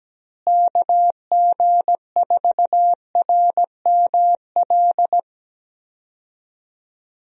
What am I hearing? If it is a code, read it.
Morse code, heard as a keyed tone, about 17 wpm, 700 Hz. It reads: KG4RML
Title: USA Call Signs - 6 Characters - Repeated 2x - 17wpm